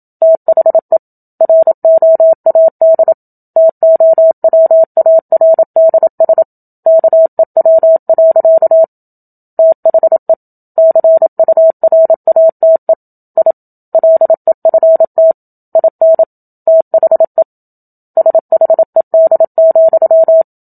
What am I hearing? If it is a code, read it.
Morse code, heard as a keyed tone, about 27 wpm, 650 Hz. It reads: THE ROAD TOWARDS KEW. THE CURATE I LEFT IN THE SHED,
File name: Book Two / 01-Chapter 1 - 27 WPM